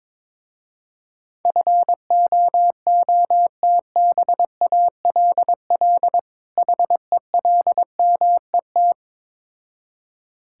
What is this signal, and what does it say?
Morse code, heard as a keyed tone, about 21 wpm, 700 Hz. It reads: FOOTBALL HELMET
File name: Difficult Words with Hints - 22wpm